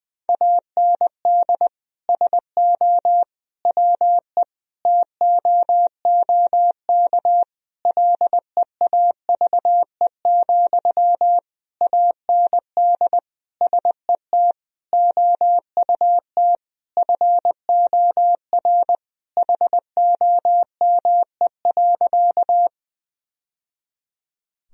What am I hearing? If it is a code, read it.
Morse code, heard as a keyed tone, about 21 wpm, 700 Hz. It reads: AND SO WE TOOK LEAVE, AND SET OUT FOR HOME.
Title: Sentences from Top 1000 Words - Review Part 1 - 20wpm